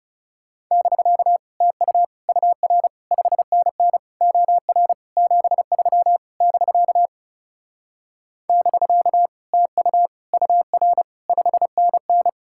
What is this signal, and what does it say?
Morse code, heard as a keyed tone, about 38 wpm, 700 Hz. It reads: <BK> TU UR 5NN OR 73 <BK> <BK> TU UR 5NN
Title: Speed-Racing Common QSO Phrases - 30wpm